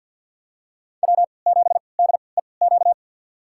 Morse code, heard as a keyed tone, about 51 wpm, spaced as for 29 wpm, 700 Hz. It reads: W7DEQ
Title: USA Call Signs - 3x Character Spacing - 50wpm